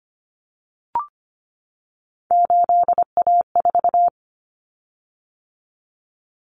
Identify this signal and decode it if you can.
Morse code, heard as a keyed tone, about 25 wpm, 700 Hz. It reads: 8A4
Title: Common Cut Numbers - 3-Digit 25wpm